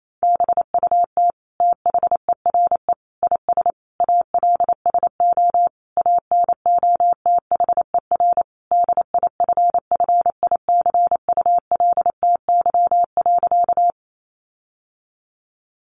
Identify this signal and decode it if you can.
Morse code, heard as a keyed tone, about 28 wpm, 700 Hz. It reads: BUT THERE IS ALSO ANOTHER DIFFICULTY.
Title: Sentences from Top 1500 Words 28wpm